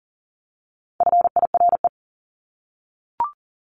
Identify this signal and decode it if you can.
Morse code, heard as a keyed tone, about 40 wpm, 700 Hz. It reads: FIRE
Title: Top 500 Words 40wpm